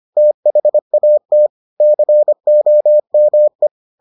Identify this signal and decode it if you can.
Morse code, heard as a keyed tone, about 25 wpm, 600 Hz. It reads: THAT COME